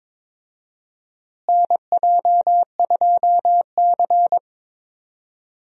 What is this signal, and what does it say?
Morse code, heard as a keyed tone, about 21 wpm, 700 Hz. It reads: NJ2C
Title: USA Call Signs - 4 Characters - Repeated 3x - 22wpm